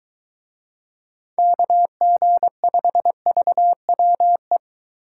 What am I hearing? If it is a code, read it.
Morse code, heard as a keyed tone, about 21 wpm, 700 Hz. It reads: KG5VWE